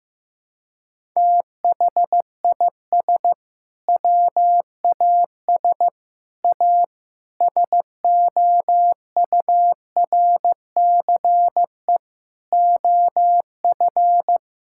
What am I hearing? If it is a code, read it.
Morse code, heard as a keyed tone, about 16 wpm, 700 Hz. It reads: THIS WAS A SOURCE OF
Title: Sentences from Top 1000 Words - Review Part 5 - 15wpm